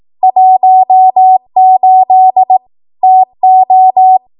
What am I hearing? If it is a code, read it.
Morse code, heard as a keyed tone, about 18 wpm, 750 Hz. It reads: 18 TO
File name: Archive / 18 WPM / 18WPM